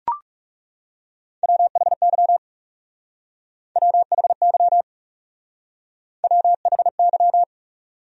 Morse code, heard as a keyed tone, about 39 wpm, 700 Hz. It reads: WHY WHY WHY